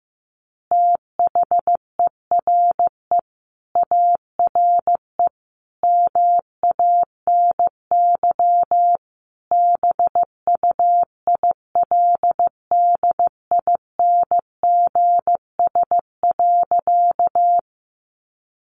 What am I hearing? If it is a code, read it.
Morse code, heard as a keyed tone, about 15 wpm, 700 Hz. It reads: THERE ARE MANY BUILDINGS.
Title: Easy Phrases 15wpm